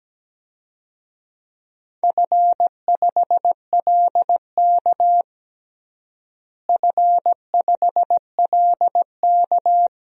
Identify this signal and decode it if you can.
Morse code, heard as a keyed tone, about 17 wpm, 700 Hz. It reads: F5LK F5LK
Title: DXCC-100 Call Signs - 4 Characters - Repeated 3x - 17wpm